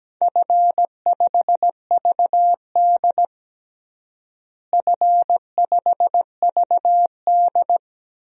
Morse code, heard as a keyed tone, about 17 wpm, 700 Hz. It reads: F5VD F5VD
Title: DXCC-100 Call Signs - 4 Characters - Repeated 2x - 17wpm